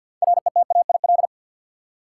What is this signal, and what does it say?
Morse code, heard as a keyed tone, about 51 wpm, 700 Hz. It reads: RETAIL
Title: Practice QSO - Exchange 3 - More Info - 50wpm